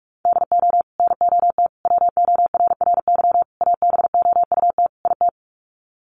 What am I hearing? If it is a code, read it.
Morse code, heard as a keyed tone, about 44 wpm, 700 Hz. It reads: DO NOT WORRY ABOUT IT
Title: Sets of 5 Words 45wpm